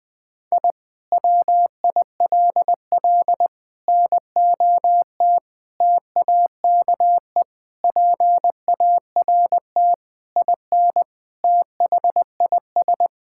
Morse code, heard as a keyed tone, about 20 wpm, 700 Hz. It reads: I WILL NOT TAKE PART IN THIS
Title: Sentences from Top 1000 Words - Review Part 6 - 20wpm